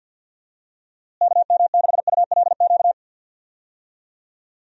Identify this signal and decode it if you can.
Morse code, heard as a keyed tone, about 47 wpm, 700 Hz. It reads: KM6ULQ
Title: USA Call Signs - 6 Characters - Repeated 3x - 50wpm